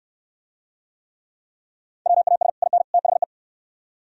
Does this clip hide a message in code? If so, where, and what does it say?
Yes: Morse code from 2.1 to 3.2 s, heard as a keyed tone, about 68 wpm, 700 Hz. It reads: WAS IN THE